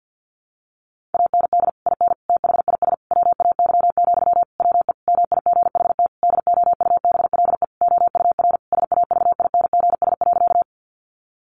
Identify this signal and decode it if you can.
Morse code, heard as a keyed tone, about 50 wpm, 700 Hz. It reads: AND IN THIS WAY, WE MIGHT DOUBLE OUR SAVINGS.
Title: Sentences from Top 1400 Words 50wpm